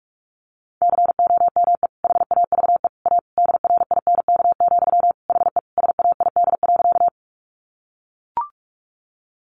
Morse code, heard as a keyed tone, about 44 wpm, 700 Hz. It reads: COME HAVE A DRINK, HE SAID.
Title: Sentences from Top 1500 Words 45wpm